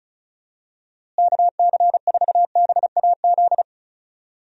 Morse code, heard as a keyed tone, about 36 wpm, 700 Hz. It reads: KC4BAZ
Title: USA Call Signs - 6 Characters - Repeated 3x - 35wpm